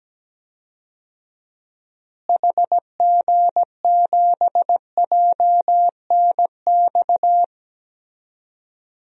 Morse code, heard as a keyed tone, about 17 wpm, 700 Hz. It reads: HG7JNX
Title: DXCC-100 Call Signs - 6 Characters - Repeated 3x - 17wpm